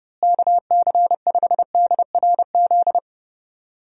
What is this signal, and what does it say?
Morse code, heard as a keyed tone, about 30 wpm, 700 Hz. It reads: KC5DRZ
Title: USA Call Signs - 6 Characters - Repeated 2x - 30wpm